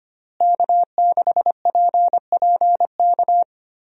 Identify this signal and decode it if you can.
Morse code, heard as a keyed tone, about 25 wpm, 700 Hz. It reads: K6PPK
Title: USA Call Signs - 5 Characters - Repeated 3x - 25wpm